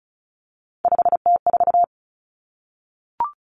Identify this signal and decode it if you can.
Morse code, heard as a keyed tone, about 36 wpm, 700 Hz. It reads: 5T4
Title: Three Challenging Letter-Number Combinations 35wpm